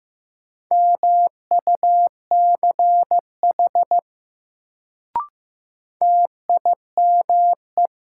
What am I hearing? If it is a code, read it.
Morse code, heard as a keyed tone, about 15 wpm, 700 Hz. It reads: MUCH TIME
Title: Sets of 2 Words - New Word Drill - 15wpm